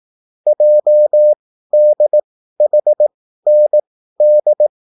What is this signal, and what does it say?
Morse code, heard as a keyed tone, about 18 wpm, spaced as for 13 wpm, 600 Hz. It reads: JDHND
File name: QSL / CW lesson